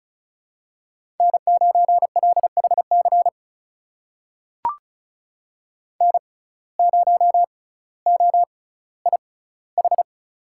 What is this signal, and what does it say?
Morse code, heard as a keyed tone, about 36 wpm, spaced as for 18 wpm, 700 Hz. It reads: N9LHC N0OIH
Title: USA Call Signs - 5 Characters - 6x Character Spacing - 35wpm